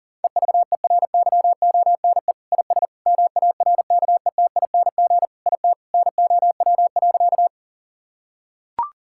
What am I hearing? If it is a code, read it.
Morse code, heard as a keyed tone, about 40 wpm, 700 Hz. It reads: EVERYONE IS MARKETING IT NOW.
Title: Sentences from Top 1000 Words - Review Part 5 - 40wpm